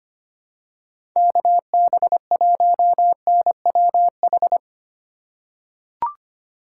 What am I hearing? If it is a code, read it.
Morse code, heard as a keyed tone, about 25 wpm, 700 Hz. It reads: KB1NWH
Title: USA Call Signs - 6 Characters - 2x Character Spacing - 25wpm